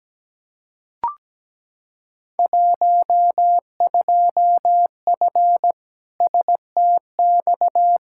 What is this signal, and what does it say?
Morse code, heard as a keyed tone, about 17 wpm, 700 Hz. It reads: 12F STX
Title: Field Day Exchanges Encore - 17wpm